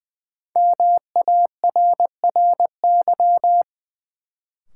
Morse code, heard as a keyed tone, about 20 wpm, 700 Hz. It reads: MARRY
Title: Sentences from Top 1200 Words - New Word Drill - 20wpm